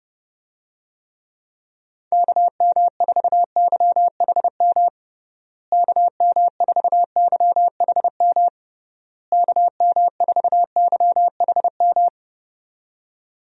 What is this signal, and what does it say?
Morse code, heard as a keyed tone, about 32 wpm, 700 Hz. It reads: KM4YHM KM4YHM KM4YHM
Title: USA Call Signs - 6 Characters - Repeated 3x - 30wpm